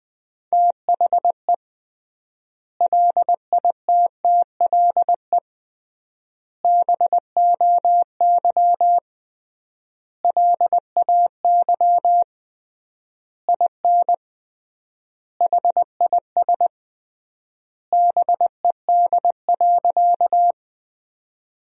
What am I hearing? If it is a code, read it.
Morse code, heard as a keyed tone, about 20 wpm, 700 Hz. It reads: THE LITTLE BOY LAY IN HIS BED.
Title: 3x Word-Spacing - Sentences from Top 500 Words - 20wpm